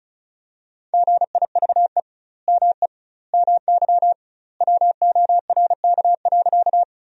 Morse code, heard as a keyed tone, about 34 wpm, 700 Hz. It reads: GIVE ME MY WORK.
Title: 2x Word-Spacing - Sentences from Top 200 Words - 35wpm